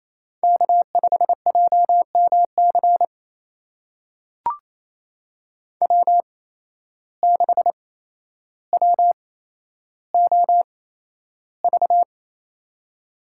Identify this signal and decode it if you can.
Morse code, heard as a keyed tone, about 28 wpm, spaced as for 12 wpm, 700 Hz. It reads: K5JMC W6WOV
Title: USA Call Signs - 5 Characters - 8x Character Spacing - 28wpm